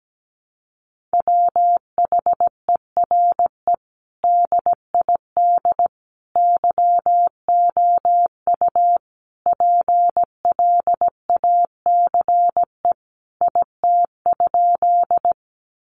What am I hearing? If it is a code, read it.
Morse code, heard as a keyed tone, about 17 wpm, 700 Hz. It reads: WHERE DID YOU PLACE IT?
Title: Sentences from Top 200 Words 17wpm